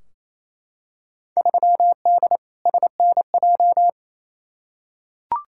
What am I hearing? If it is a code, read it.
Morse code, heard as a keyed tone, about 28 wpm, 700 Hz. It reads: 3D SNJ
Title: Field Day Exchanges Encore - 28wpm